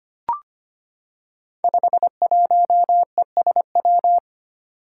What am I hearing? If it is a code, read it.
Morse code, heard as a keyed tone, about 25 wpm, 700 Hz. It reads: 51ESW
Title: Five Challenging Letter-Number Combinations 25wpm